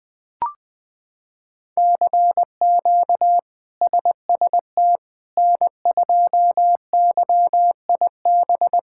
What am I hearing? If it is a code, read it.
Morse code, heard as a keyed tone, about 20 wpm, 700 Hz. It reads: CQ SST N2YIB
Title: K1USN SST QSO Practice - 20wpm